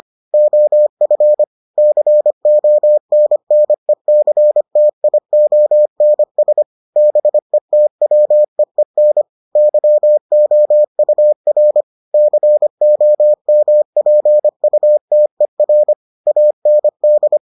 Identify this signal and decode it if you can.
Morse code, heard as a keyed tone, about 25 wpm, 600 Hz. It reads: OF CONNECTIONS BETWEEN YOUR COMPUTER AND